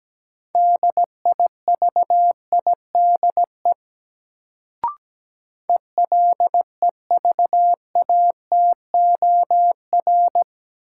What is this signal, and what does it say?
Morse code, heard as a keyed tone, about 17 wpm, 700 Hz. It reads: DIVIDE ELEVATOR
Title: Sentences from Top 1600 Words - New Word Drill Alt - 17wpm